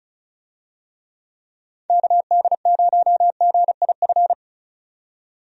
Morse code, heard as a keyed tone, about 36 wpm, 700 Hz. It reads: KD0GIF
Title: USA Call Signs - 6 Characters - 35wpm